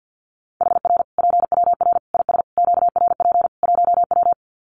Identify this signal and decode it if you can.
Morse code, heard as a keyed tone, about 50 wpm, 700 Hz. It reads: HR PWR IS QRP 1W
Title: Common QSO Phrases 50wpm